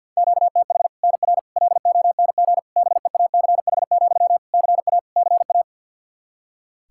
Morse code, heard as a keyed tone, about 51 wpm, 700 Hz. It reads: QTH NR LONG BEACH, CA CA
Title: K1USN SST QSO Practice 50wpm